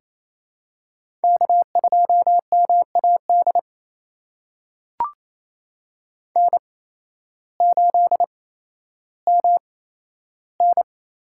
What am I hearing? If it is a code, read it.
Morse code, heard as a keyed tone, about 28 wpm, spaced as for 13 wpm, 700 Hz. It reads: K2MAD N8MN
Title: USA Call Signs - 5 Characters - 8x Character Spacing - 28wpm